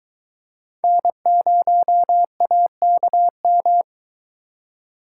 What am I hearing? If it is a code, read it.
Morse code, heard as a keyed tone, about 24 wpm, 700 Hz. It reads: N0AKM